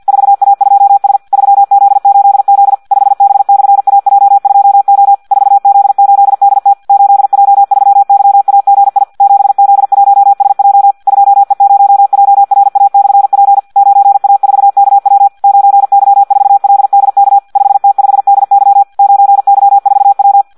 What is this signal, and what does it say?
Morse code, heard as a keyed tone, about 50 wpm, 800 Hz. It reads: 4A2S 3Z8Z 5B/AJ2O 4Z8DT 8J3YAGI 7Z1SJ 2E0JRA/P 9A4KW 9Y4LDK 5T5DY 9Y4W
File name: Audio practica CW / qrq-tx-50 2